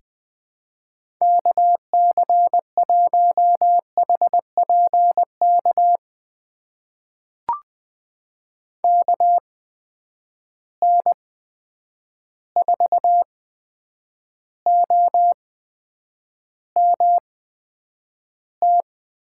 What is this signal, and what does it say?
Morse code, heard as a keyed tone, about 19 wpm, spaced as for 8 wpm, 700 Hz. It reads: KC1HPK KN4OMT
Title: USA Call Signs - 6 Characters - 8x Character Spacing - 20wpm